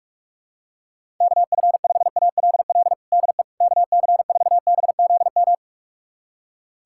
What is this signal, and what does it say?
Morse code, heard as a keyed tone, about 46 wpm, 700 Hz. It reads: KF5ALL DE KC4BZM
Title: USA Callsign DE Callsign - 6 Characters - 45wpm